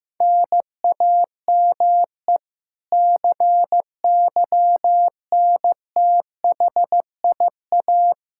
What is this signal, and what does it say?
Morse code, heard as a keyed tone, about 15 wpm, 700 Hz. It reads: NAME CYNTHIA
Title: K1USN SST QSO Practice 15wpm